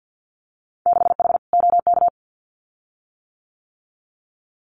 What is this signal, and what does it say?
Morse code, heard as a keyed tone, about 50 wpm, 700 Hz. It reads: 6H OK
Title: Winter Field Day Exchanges Encore 50wpm